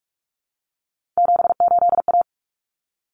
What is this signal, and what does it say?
Morse code, heard as a keyed tone, about 44 wpm, 700 Hz. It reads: 78A
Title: Common Cut Numbers - 3-Digit 45wpm